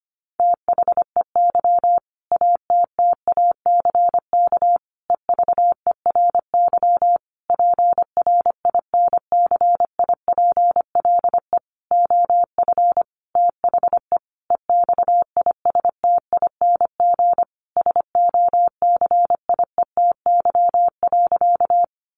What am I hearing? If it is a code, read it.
Morse code, heard as a keyed tone, about 25 wpm, 700 Hz. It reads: THEY ATTACK EVERY PRINCIPLE OF THE EXISTING SOCIETY.
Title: Sentences from Top 1500 Words 25wpm